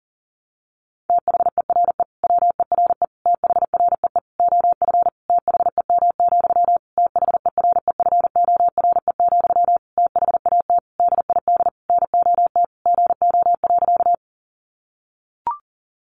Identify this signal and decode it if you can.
Morse code, heard as a keyed tone, about 40 wpm, 700 Hz. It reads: THERE WERE THREE OF THEM, THEREFORE, THAT DID NOT GO.
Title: Sentences from Top 800 Words 40wpm